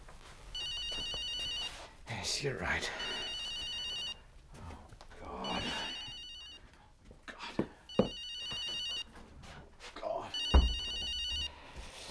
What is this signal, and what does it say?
Speech. There is no code.